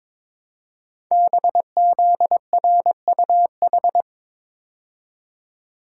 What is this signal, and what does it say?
Morse code, heard as a keyed tone, about 23 wpm, 700 Hz. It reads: BZRUH
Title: Code Groups - 5 Letters - 22wpm